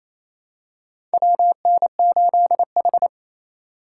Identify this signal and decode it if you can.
Morse code, heard as a keyed tone, about 28 wpm, 700 Hz. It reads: WN8H
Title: USA Call Signs - 4 Characters - 2x Character Spacing - 28wpm